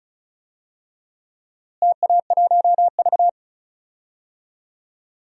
Morse code, heard as a keyed tone, about 36 wpm, 700 Hz. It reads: TA1V
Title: DXCC-100 Call Signs - 4 Characters - Repeated 3x - 35wpm